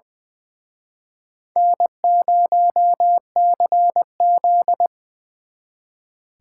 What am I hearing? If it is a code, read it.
Morse code, heard as a keyed tone, about 24 wpm, 700 Hz. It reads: N0CZ
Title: Speed Racing - USA 1x2 Call Signs 20wpm